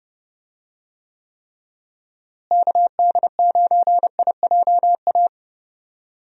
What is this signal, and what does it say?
Morse code, heard as a keyed tone, about 30 wpm, 700 Hz. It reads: KD9IJA
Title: USA Call Signs - 6 Characters - Repeated 3x - 30wpm